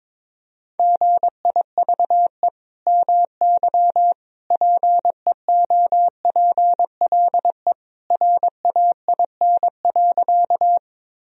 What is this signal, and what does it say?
Morse code, heard as a keyed tone, about 22 wpm, 700 Hz. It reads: GIVE MY PEOPLE RAIN.
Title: Sentences from Top 1000 Words - Review Part 1 - 22wpm